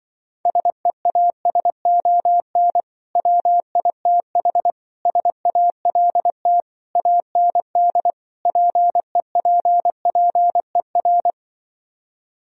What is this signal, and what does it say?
Morse code, heard as a keyed tone, about 24 wpm, 700 Hz. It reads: SEASON WITH SALT AND PEPPER
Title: Speed-Racing - Sets of 5 Words 20wpm